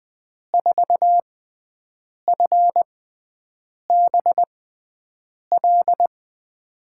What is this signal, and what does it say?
Morse code, heard as a keyed tone, about 20 wpm, spaced as for 7 wpm, 700 Hz. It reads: 4FBL
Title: USA Call Signs - 6 Characters - 6x Character Spacing - 20wpm